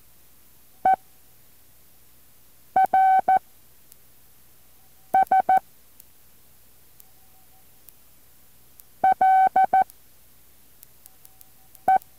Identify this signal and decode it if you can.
Morse code, heard as a keyed tone, about 14 wpm, 760 Hz. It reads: E R S L E
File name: intl morse code2